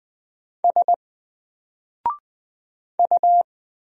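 Morse code, heard as a keyed tone, about 20 wpm, 700 Hz. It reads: S U
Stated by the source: Single letter and number Morse drill at 20 wpm, from N4LCD.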